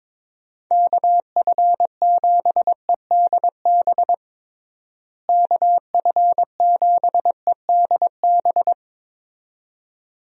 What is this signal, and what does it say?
Morse code, heard as a keyed tone, about 22 wpm, 700 Hz. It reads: KF7EDB KF7EDB
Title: USA Call Signs - 6 Characters - Repeated 2x - 22wpm